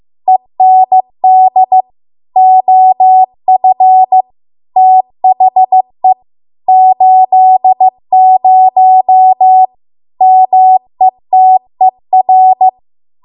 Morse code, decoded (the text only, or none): END OF THE 80 METER